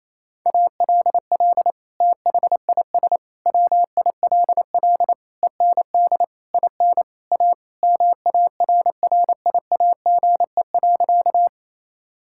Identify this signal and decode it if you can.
Morse code, heard as a keyed tone, about 28 wpm, 700 Hz. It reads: ALL THIS WILL END IN A MARRIAGE.
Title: Sentences from Top 1000 Words - Review Part 5 - 28wpm